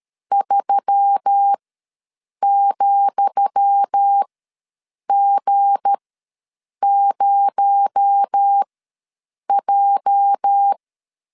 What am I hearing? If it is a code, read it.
Morse code, heard as a keyed tone, about 13 wpm, spaced as for 7 wpm, 800 Hz. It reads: 3,G0J
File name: LEZIONI / 10 SIGNS / 10D COMA ws 12wpm 3618